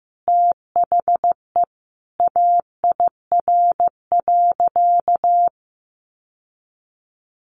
Morse code, heard as a keyed tone, about 15 wpm, 700 Hz. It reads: THE AIR.
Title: Sentences from Top 600 Words 15wpm